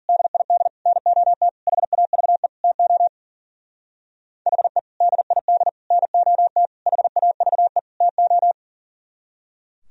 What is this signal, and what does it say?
Morse code, heard as a keyed tone, about 50 wpm, 700 Hz. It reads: DID NOT HAVE TO HE DID NOT HAVE TO